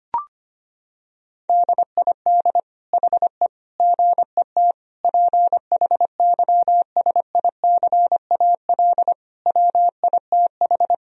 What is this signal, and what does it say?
Morse code, heard as a keyed tone, about 25 wpm, 700 Hz. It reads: DID HE GET PHYSICAL WITH
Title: Sentences from Top 1000 Words - Review Part 4 - 25wpm